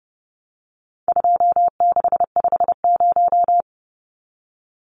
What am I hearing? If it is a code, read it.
Morse code, heard as a keyed tone, about 30 wpm, 700 Hz. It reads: 2650
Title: Numbers - 4-Digit 30wpm